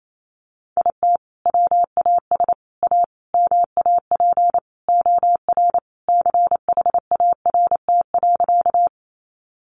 Morse code, heard as a keyed tone, about 28 wpm, 700 Hz. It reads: IT WAS A MAP OR CHART.
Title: Sentences from Top 1300 Words 28wpm